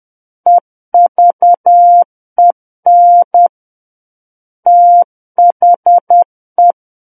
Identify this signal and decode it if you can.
Morse code, heard as a keyed tone, about 10 wpm, 700 Hz. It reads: EVEN THE
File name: Book One / 01-Chapter 1 - 10 WPM